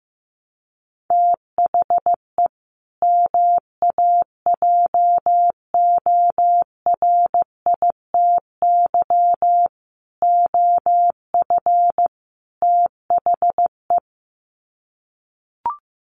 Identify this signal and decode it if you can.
Morse code, heard as a keyed tone, about 15 wpm, 700 Hz. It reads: THE MAJORITY OF THE
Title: Sets of 4 Words 15wpm